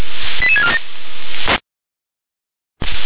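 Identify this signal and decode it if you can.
Five-tone selective call (ZVEI), heard as quick runs of five tones, tones of about 70 ms. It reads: 88647
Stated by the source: ZVEI five-tone selective call received on an SDR.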